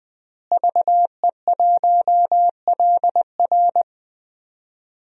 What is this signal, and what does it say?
Morse code, heard as a keyed tone, about 20 wpm, 700 Hz. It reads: VE1LR
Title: DXCC-100 Call Signs - 5 Characters - Repeated 2x - 20wpm